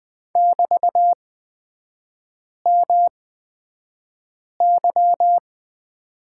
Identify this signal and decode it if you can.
Morse code, heard as a keyed tone, about 20 wpm, spaced as for 5 wpm, 690 Hz. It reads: <BT>MY
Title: 236 Lesson 38e - Letter q - Three Characters 20wpm